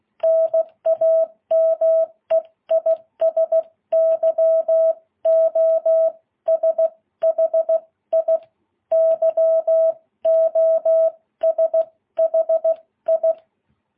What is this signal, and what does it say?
Morse code, heard as a keyed tone, about 16 wpm, spaced as for 14 wpm, 660 Hz. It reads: NAME IS YOSHI YOSHI
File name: jh8cbh